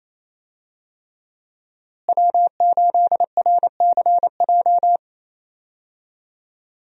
Morse code, heard as a keyed tone, about 27 wpm, 700 Hz. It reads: W8RCJ
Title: USA Call Signs - 5 Characters - Repeated 3x - 28wpm